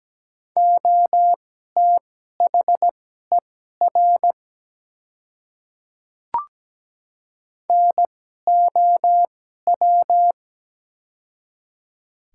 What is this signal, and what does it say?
Morse code, heard as a keyed tone, about 17 wpm, spaced as for 12 wpm, 700 Hz. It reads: OTHER NOW
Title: 2x Character Spacing - Top 100 Words - 17wpm